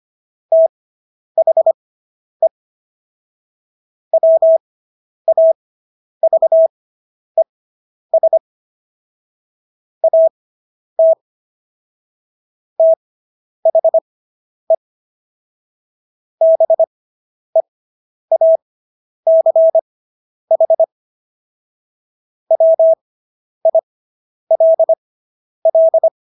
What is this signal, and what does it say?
Morse code, heard as a keyed tone, about 25 wpm, spaced as for 10 wpm, 650 Hz. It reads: THE WAVES AT THE BEACH WILL